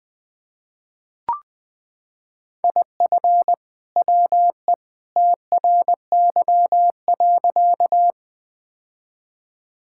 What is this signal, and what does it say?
Morse code, heard as a keyed tone, about 20 wpm, 700 Hz. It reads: IF WE TRY.
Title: Sentences with 2-Word Phrases - 20wpm